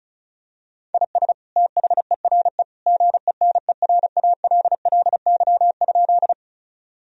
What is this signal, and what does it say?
Morse code, heard as a keyed tone, about 34 wpm, 700 Hz. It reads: IS THERE GENERALLY?
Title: Sentences from Top 1000 Words - Review Part 4 - 35wpm